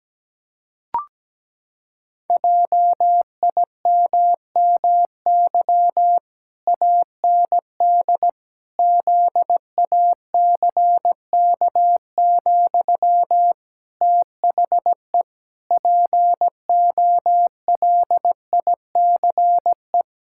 Morse code, heard as a keyed tone, about 17 wpm, 700 Hz. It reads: JIMMY AND ZACK, THE POLICE
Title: Pangram Challenge v2 - 17wpm